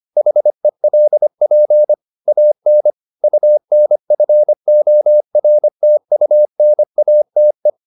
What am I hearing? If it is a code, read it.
Morse code, heard as a keyed tone, about 25 wpm, 600 Hz. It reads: HELP AN UNFORTUNATE